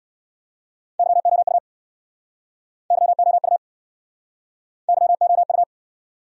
Morse code, heard as a keyed tone, about 67 wpm, 700 Hz. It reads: YOU YOU YOU